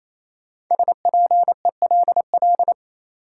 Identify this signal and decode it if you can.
Morse code, heard as a keyed tone, about 28 wpm, 700 Hz. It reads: SPELL
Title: Sentences from Top 1000 Words - Review Part 1 - New Word Drill - 28wpm